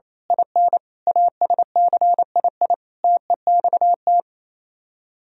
Morse code, heard as a keyed tone, about 27 wpm, 700 Hz. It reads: IN ASCII TEXT
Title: Speed-Racing - Sets of 5 Words 28wpm